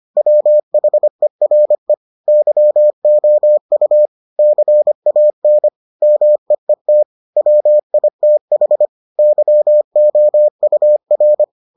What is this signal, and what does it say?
Morse code, heard as a keyed tone, about 25 wpm, 600 Hz. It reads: WHERE YOU CAN MEET WITH YOUR